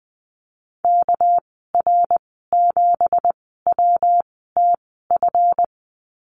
Morse code, heard as a keyed tone, about 19 wpm, spaced as for 14 wpm, 700 Hz. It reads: KR7WTF
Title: 219 Lesson 35g - Number 0 - Call Signs 20wpm